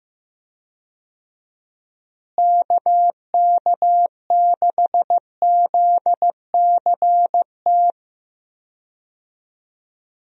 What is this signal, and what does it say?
Morse code, heard as a keyed tone, about 15 wpm, 700 Hz. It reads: KK6ZCT
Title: USA Call Signs - 6 Characters - Repeated 3x - 15wpm